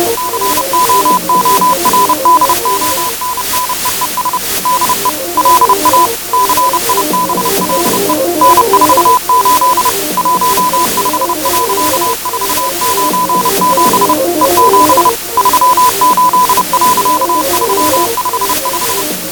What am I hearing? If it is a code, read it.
Morse code, heard as a keyed tone, about 30 wpm, 1000 Hz. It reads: COORDONNEES DE LA CIBLE N48 15.497 E7 38.355